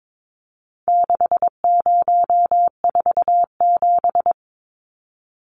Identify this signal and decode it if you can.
Morse code, heard as a keyed tone, about 22 wpm, 700 Hz. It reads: 6047
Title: Numbers - 4-Digit 22wpm